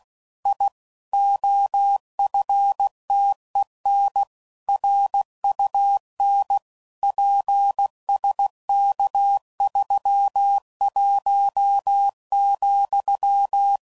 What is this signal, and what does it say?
Morse code, heard as a keyed tone, about 16 wpm, 800 Hz. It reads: I OFTEN RUN PSK31,
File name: CW Code Groups @15WPM / QSOs / kc7haa